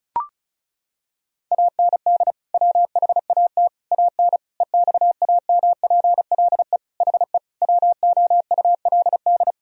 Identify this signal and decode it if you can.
Morse code, heard as a keyed tone, about 36 wpm, 700 Hz. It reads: AND WHAT AN EXAMPLE HE WOULD
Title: Sentences from Top 1000 Words - Review Part 1 - 35wpm